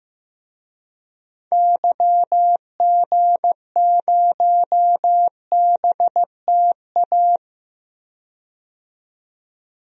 Morse code, heard as a keyed tone, about 15 wpm, 700 Hz. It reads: YG0BTA
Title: DXCC-100 Call Signs - 6 Characters - 15wpm